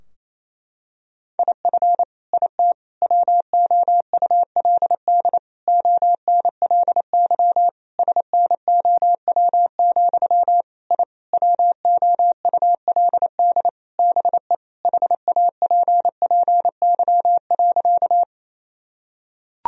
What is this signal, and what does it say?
Morse code, heard as a keyed tone, about 28 wpm, 700 Hz. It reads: IF IT WOULD ONLY SNOW, I WOULD BE HAPPY.
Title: Sentences from Top 1000 Words - Review Part 3 - 28wpm